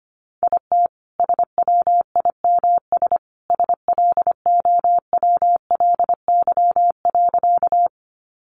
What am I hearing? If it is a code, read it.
Morse code, heard as a keyed tone, about 25 wpm, 700 Hz. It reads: IT SWIMS SLOWLY.
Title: Additional Easy Phrases 25wpm